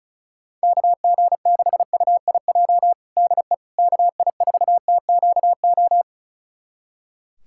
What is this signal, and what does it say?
Morse code, heard as a keyed tone, about 36 wpm, 700 Hz. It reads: KG6UIJ DE KI4TQO
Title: USA Callsign DE Callsign - 6 Characters - 35wpm